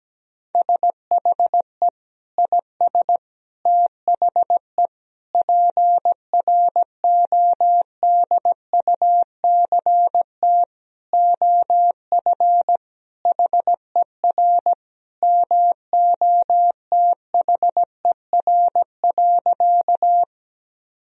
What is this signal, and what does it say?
Morse code, heard as a keyed tone, about 17 wpm, 700 Hz. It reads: SHE IS THE PRODUCT OF HER MOTHER.
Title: Sentences from Top 1000 Words - Review Part 2 - 17wpm